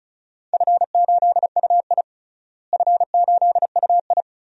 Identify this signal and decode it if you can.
Morse code, heard as a keyed tone, about 36 wpm, 690 Hz. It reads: F8UI F8UI
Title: DXCC-100 Call Signs - 4 Characters - Repeated 2x - 35wpm